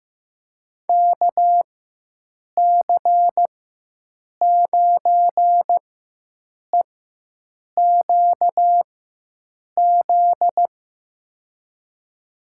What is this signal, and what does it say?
Morse code, heard as a keyed tone, about 15 wpm, spaced as for 7 wpm, 700 Hz. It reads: KC9EQZ
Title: USA Call Signs - 6 Characters - 4x Character Spacing - 15wpm